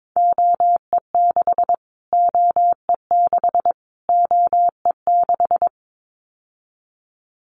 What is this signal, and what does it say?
Morse code, heard as a keyed tone, about 22 wpm, 700 Hz. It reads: OE6 OE6 OE6
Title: Callsign Prefixes for Top 25 DXCC - Repeated 3x - 22wpm